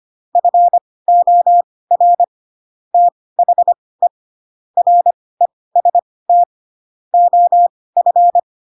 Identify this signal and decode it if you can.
Morse code, heard as a keyed tone, about 25 wpm, spaced as for 18 wpm, 700 Hz. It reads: FOR THE REST OF
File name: Pinocchio / Pinocchio ch2 12 25 ik1hge